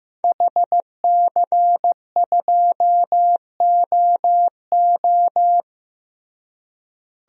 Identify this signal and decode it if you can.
Morse code, heard as a keyed tone, about 15 wpm, 700 Hz. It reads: HC2OO
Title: DXCC-100 Call Signs - 5 Characters - Repeated 2x - 15wpm